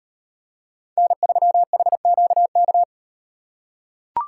Morse code, heard as a keyed tone, about 36 wpm, 700 Hz. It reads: N3HQK